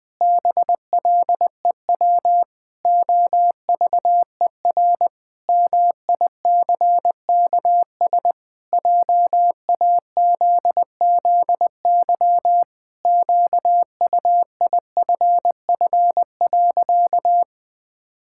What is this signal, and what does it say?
Morse code, heard as a keyed tone, about 20 wpm, 700 Hz. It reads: BLEW OVER MICKS JAZZY QUIFF.
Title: Pangram Challenge v2 - 20wpm